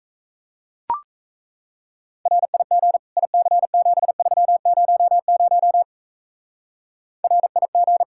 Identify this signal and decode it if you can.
Morse code, heard as a keyed tone, about 36 wpm, 700 Hz. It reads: RIG IC7300 RIG